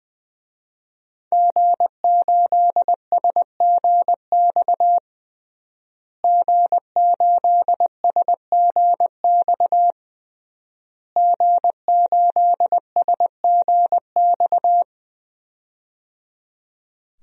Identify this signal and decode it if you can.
Morse code, heard as a keyed tone, about 20 wpm, 700 Hz. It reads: G8SGX G8SGX G8SGX
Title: DXCC-100 Call Signs - 5 Characters - Repeated 3x - 20wpm